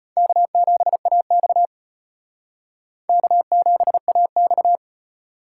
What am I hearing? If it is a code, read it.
Morse code, heard as a keyed tone, about 32 wpm, 700 Hz. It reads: K7AX K7AX